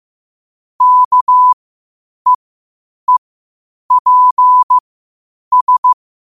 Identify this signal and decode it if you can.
Morse code, heard as a keyed tone, about 15 wpm, spaced as for 8 wpm, 1000 Hz. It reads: KEEPS
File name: qotdfiles / 8wpm